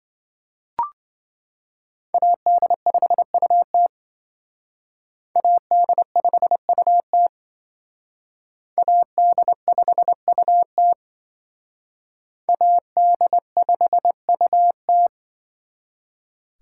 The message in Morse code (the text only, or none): AD5UT AD5UT AD5UT AD5UT